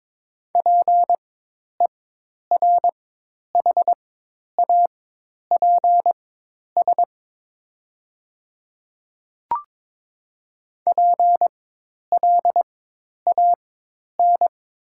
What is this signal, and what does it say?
Morse code, heard as a keyed tone, about 23 wpm, spaced as for 10 wpm, 700 Hz. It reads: PERHAPS PLAN
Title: 4x Character Spacing - Top 500 Words - 22wpm